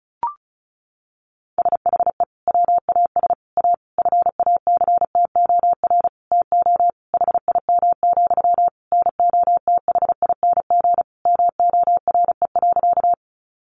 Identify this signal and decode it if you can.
Morse code, heard as a keyed tone, about 34 wpm, 700 Hz. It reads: SHE WAS A FACTOR TO HIM, NOTHING MORE.
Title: Sentences from Top 1300 Words 35wpm